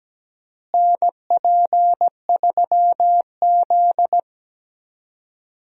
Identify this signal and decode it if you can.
Morse code, heard as a keyed tone, about 17 wpm, 700 Hz. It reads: NP3Z
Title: USA Call Signs - 4 Characters - Repeated 3x - 17wpm